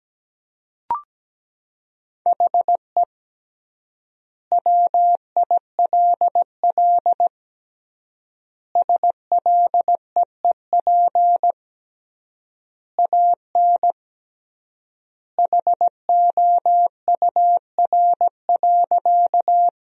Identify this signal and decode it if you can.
Morse code, heard as a keyed tone, about 17 wpm, 700 Hz. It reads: HE WILL SLEEP AN HOUR.
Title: 3x Word-Spacing - Sentences from Top 500 Words - 17wpm